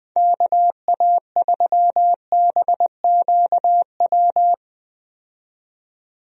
Morse code, heard as a keyed tone, about 20 wpm, 700 Hz. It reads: KA3BQW
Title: USA Call Signs - 6 Characters - 20wpm